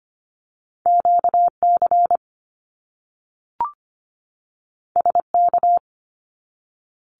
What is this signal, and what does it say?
Morse code, heard as a keyed tone, about 25 wpm, 700 Hz. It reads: QC SK
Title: Canadian Provinces and Territories 25wpm